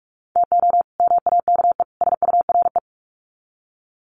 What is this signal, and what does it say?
Morse code, heard as a keyed tone, about 44 wpm, 700 Hz. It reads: TO MAKE SURE
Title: Sets of 3 Words 45wpm